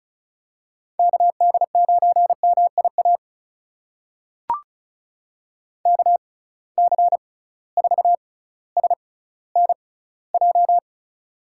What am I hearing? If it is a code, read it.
Morse code, heard as a keyed tone, about 36 wpm, spaced as for 18 wpm, 700 Hz. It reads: KD9MIA KC4SNJ
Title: USA Call Signs - 6 Characters - 6x Character Spacing - 35wpm